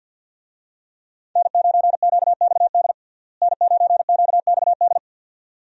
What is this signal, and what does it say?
Morse code, heard as a keyed tone, about 47 wpm, 700 Hz. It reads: N9QXD N9QXD
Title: USA Call Signs - 5 Characters - Repeated 2x - 50wpm